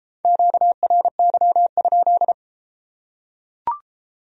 Morse code, heard as a keyed tone, about 32 wpm, 700 Hz. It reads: QRY?